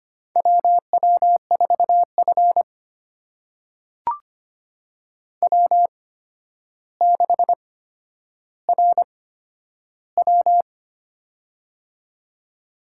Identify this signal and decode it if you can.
Morse code, heard as a keyed tone, about 25 wpm, spaced as for 11 wpm, 700 Hz. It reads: WW4F W6RW
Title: USA Call Signs - 4 Characters - 8x Character Spacing - 25wpm